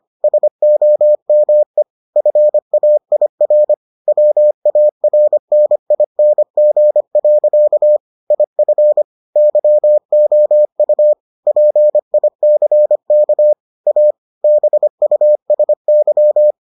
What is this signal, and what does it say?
Morse code, heard as a keyed tone, about 25 wpm, 600 Hz. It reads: SOME FAIR WARNING. IF YOU PICK A BUSY